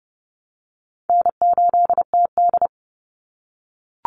Morse code, heard as a keyed tone, about 30 wpm, 700 Hz. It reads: N8TD
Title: Call Signs - 30wpm